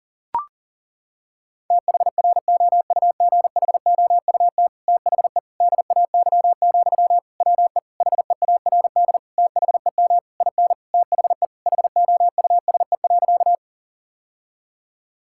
Morse code, heard as a keyed tone, about 37 wpm, 700 Hz. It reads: THROUGHOUT THE DAY, WE HEARD THEM IN THE HOUSE.
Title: Sentences from Top 1000 Words - Review Part 5 - 40wpm